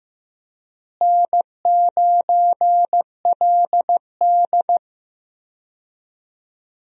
Morse code, heard as a keyed tone, about 15 wpm, 700 Hz. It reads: N9LD
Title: USA Call Signs - 4 Characters - Repeated 3x - 15wpm